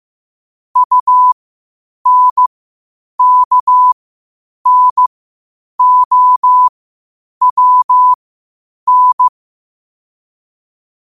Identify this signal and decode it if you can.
Morse code, heard as a keyed tone, about 15 wpm, spaced as for 8 wpm, 1000 Hz. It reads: UNKNOWN
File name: qotdfiles / 8wpm